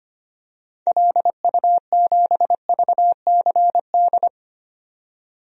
Morse code, heard as a keyed tone, about 25 wpm, 700 Hz. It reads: LU7VCD
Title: DXCC-100 Call Signs - 6 Characters - 25wpm